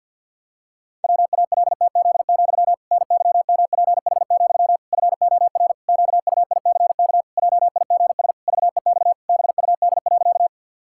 Morse code, heard as a keyed tone, about 52 wpm, 700 Hz. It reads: WALTZ, NYMPH, FOR QUICK JIGS VEX BUD.
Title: Pangram Challenge v2 - 50wpm